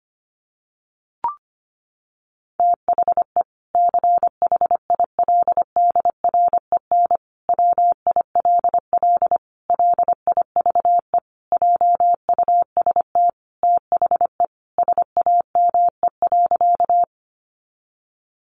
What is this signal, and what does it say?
Morse code, heard as a keyed tone, about 25 wpm, 700 Hz. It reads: THE CHILDREN WILL LIVE JUST THE SAME.
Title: Sentences from Top 300 Words 25wpm